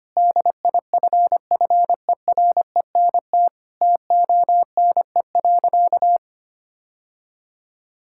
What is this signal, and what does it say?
Morse code, heard as a keyed tone, about 25 wpm, 700 Hz. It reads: DIFFERENT TONE.
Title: Sentences from Top 1600 Words 25wpm